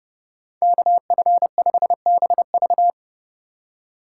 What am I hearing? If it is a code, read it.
Morse code, heard as a keyed tone, about 29 wpm, 700 Hz. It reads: KF5BV